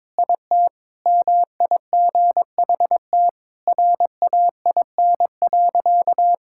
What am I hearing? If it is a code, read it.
Morse code, heard as a keyed tone, about 22 wpm, 700 Hz. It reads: IT MIGHT RAIN.
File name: practice / 22wpm / Sentences with 3-Word Phrases - 22wpm